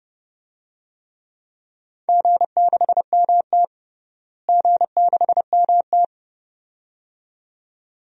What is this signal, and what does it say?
Morse code, heard as a keyed tone, about 30 wpm, 700 Hz. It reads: G6MT G6MT
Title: DXCC-100 Call Signs - 4 Characters - Repeated 2x - 30wpm